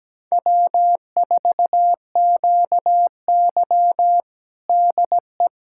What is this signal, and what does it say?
Morse code, heard as a keyed tone, about 17 wpm, 700 Hz. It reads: W4QY DE
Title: USA Callsign DE Callsign - 4 Characters - 17wpm